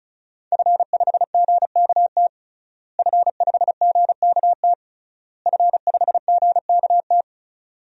Morse code, heard as a keyed tone, about 36 wpm, 700 Hz. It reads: F5GKT F5GKT F5GKT
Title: DXCC-100 Call Signs - 5 Characters - Repeated 3x - 35wpm